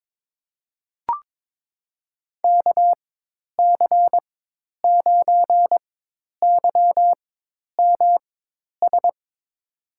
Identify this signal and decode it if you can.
Morse code, heard as a keyed tone, about 22 wpm, spaced as for 10 wpm, 700 Hz. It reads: KC9YMS
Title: USA Call Signs - 6 Characters - 4x Character Spacing - 22wpm